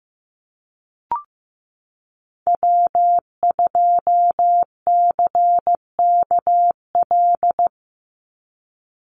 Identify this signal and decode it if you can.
Morse code, heard as a keyed tone, about 15 wpm, 700 Hz. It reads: W2CKL
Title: NA SOTA Activator Callsigns 15wpm